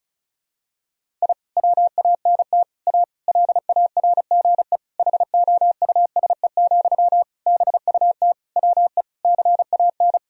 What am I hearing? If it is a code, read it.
Morse code, heard as a keyed tone, about 34 wpm, 700 Hz. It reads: I WANT A LARGE HOUSE, BUT WE CAN
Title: Sentences from Top 1000 Words - Review Part 1 - 35wpm